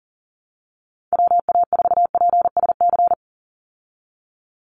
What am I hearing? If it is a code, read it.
Morse code, heard as a keyed tone, about 40 wpm, 700 Hz. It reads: WA4PSC
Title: Call Signs Repeated 2x - 40wpm